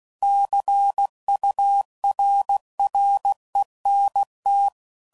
Morse code, heard as a keyed tone, about 16 wpm, 800 Hz. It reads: CURRENT
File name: CW Code Groups @15WPM / Technical Subjects / XL